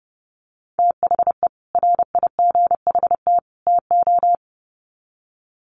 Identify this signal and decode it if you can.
Morse code, heard as a keyed tone, about 30 wpm, 700 Hz. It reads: THE RIGHT TO
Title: Sets of 3 Words 30wpm